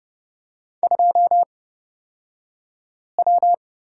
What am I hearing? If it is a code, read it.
Morse code, heard as a keyed tone, about 30 wpm, spaced as for 5 wpm, 700 Hz. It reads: 2W
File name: Lesson 37d - Letter x - Two Characters 30wpm